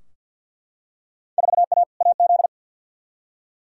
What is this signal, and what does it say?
Morse code, heard as a keyed tone, about 52 wpm, 700 Hz. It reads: 4A AZ
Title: Field Day Exchanges - 50wpm